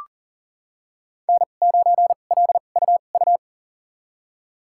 Morse code, heard as a keyed tone, about 40 wpm, spaced as for 29 wpm, 700 Hz. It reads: N9LUU
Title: USA Call Signs - 5 Characters - 2x Character Spacing - 40wpm